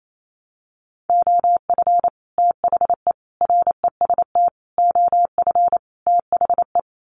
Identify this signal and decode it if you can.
Morse code, heard as a keyed tone, about 28 wpm, 700 Hz. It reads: OF THE REST OF THE
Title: Sets of 5 Words 28wpm